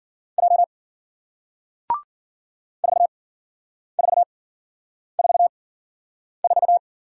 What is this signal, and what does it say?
Morse code, heard as a keyed tone, about 54 wpm, 700 Hz. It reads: J 4 4 4 4